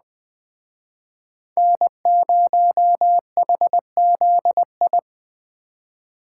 Morse code, heard as a keyed tone, about 20 wpm, 700 Hz. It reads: N0HZI
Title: Speed Racing - USA 1x3 Call Signs 17wpm